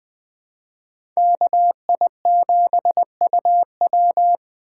Morse code, heard as a keyed tone, about 20 wpm, 700 Hz. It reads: KI7UW
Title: USA Call Signs - 5 Characters - 2x Character Spacing - 20wpm